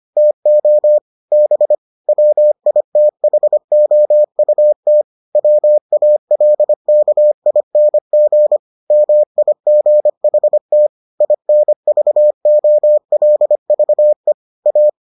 Morse code, heard as a keyed tone, about 25 wpm, 600 Hz. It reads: TO B WITHOUT WALKING MIGHT INVOLVE A